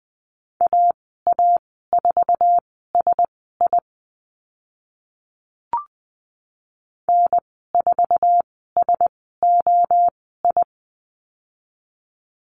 Morse code, heard as a keyed tone, about 20 wpm, spaced as for 14 wpm, 700 Hz. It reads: AA4SI N4SOI
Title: 062 Lesson 11g - Letter h - Call Signs 20wpm